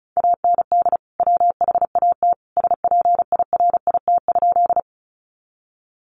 Morse code, heard as a keyed tone, about 34 wpm, 700 Hz. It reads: AND WHAT SPIRIT?
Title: Sentences from Top 1400 Words 35wpm